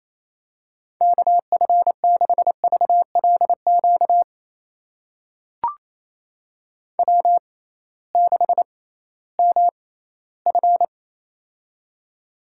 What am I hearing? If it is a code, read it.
Morse code, heard as a keyed tone, about 28 wpm, spaced as for 16 wpm, 700 Hz. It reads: KF6VLQ W6MF
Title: USA Call Signs - 6x Character Spacing - 28wpm